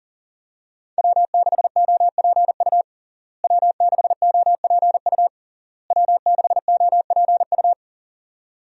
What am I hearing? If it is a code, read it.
Morse code, heard as a keyed tone, about 37 wpm, 700 Hz. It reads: W6OPU W6OPU W6OPU
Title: USA Call Signs - 5 Characters - Repeated 3x - 40wpm